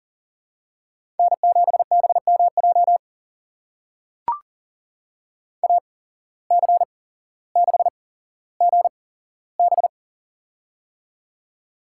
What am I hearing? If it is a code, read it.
Morse code, heard as a keyed tone, about 40 wpm, spaced as for 17 wpm, 700 Hz. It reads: N7BMJ AC6GB
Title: USA Call Signs - 5 Characters - 8x Character Spacing - 40wpm